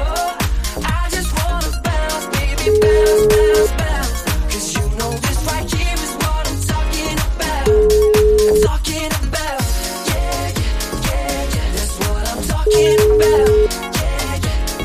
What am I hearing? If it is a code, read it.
Ringback tone (Europe, 425 Hz), with noise.